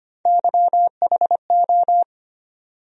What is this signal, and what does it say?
Morse code, heard as a keyed tone, about 25 wpm, 700 Hz. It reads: YHO
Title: USA Call Sign Postfixes - 3 Letters 25wpm